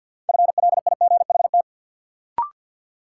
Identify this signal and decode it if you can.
Morse code, heard as a keyed tone, about 50 wpm, 700 Hz. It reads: FLIGHT
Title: Sentences from Top 1000 Words - New Word Drill - 50wpm